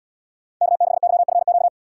Morse code, heard as a keyed tone, about 78 wpm, 700 Hz. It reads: K6YFQ